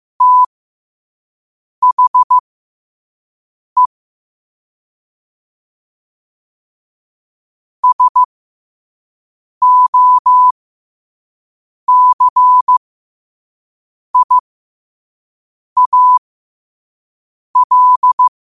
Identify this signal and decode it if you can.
Morse code, heard as a keyed tone, about 15 wpm, spaced as for 5 wpm, 1000 Hz. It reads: THE SOCIAL
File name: qotdfiles / 5wpm